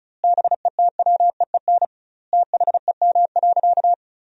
Morse code, heard as a keyed tone, about 36 wpm, 700 Hz. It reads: BETWEEN THEM.
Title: 2x Word-Spacing - Sentences from Top 400 Words - 35wpm